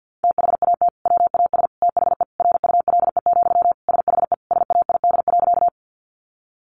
Morse code, heard as a keyed tone, about 50 wpm, 700 Hz. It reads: THAT WAS THE RULE, SHE SAID.
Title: Sentences from Top 400 Words 50wpm